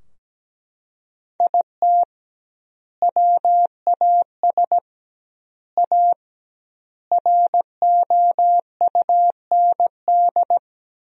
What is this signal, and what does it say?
Morse code, heard as a keyed tone, about 17 wpm, 700 Hz. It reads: IT WAS A ROUND
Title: 2x Word-Spacing - Sentences from Top 400 Words - 17wpm